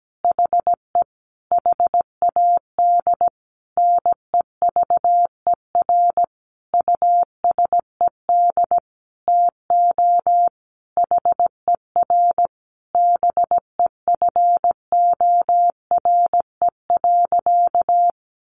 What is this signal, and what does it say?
Morse code, heard as a keyed tone, about 17 wpm, 700 Hz. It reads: HE HAD NEVER USED TO HER BEFORE.
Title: Sentences from Top 600 Words 17wpm